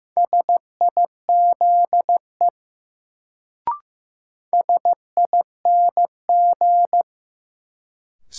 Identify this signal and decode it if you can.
Morse code, heard as a keyed tone, about 15 wpm, 700 Hz. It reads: SIZE SING
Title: Sentences from Top 400 Words - New Word Drill - 15wpm